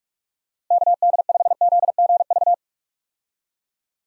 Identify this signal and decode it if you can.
Morse code, heard as a keyed tone, about 41 wpm, 700 Hz. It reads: KD5ZGV
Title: USA Call Signs - 6 Characters - Repeated 2x - 45wpm